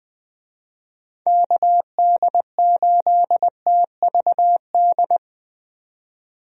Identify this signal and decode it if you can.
Morse code, heard as a keyed tone, about 21 wpm, 700 Hz. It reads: KD8TVD
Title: USA Call Signs - 6 Characters - 20wpm